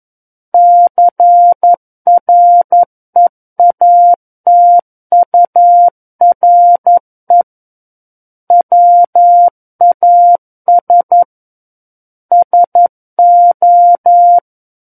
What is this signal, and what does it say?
Morse code, heard as a keyed tone, about 11 wpm, 700 Hz. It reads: CREATURE WAS SO
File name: 11 WPM